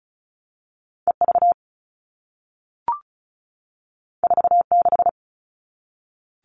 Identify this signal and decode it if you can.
Morse code, heard as a keyed tone, about 36 wpm, 700 Hz. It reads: EV 46
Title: All Cut Numbers - 2-Digit 35wpm